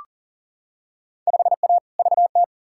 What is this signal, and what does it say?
Morse code, heard as a keyed tone, about 40 wpm, 700 Hz. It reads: 5A VT
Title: Field Day Exchanges Encore - 40wpm